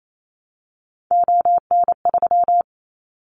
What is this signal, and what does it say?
Morse code, heard as a keyed tone, about 28 wpm, 700 Hz. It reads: ON3
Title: Callsign Prefixes for Top 100 DXCC - 28wpm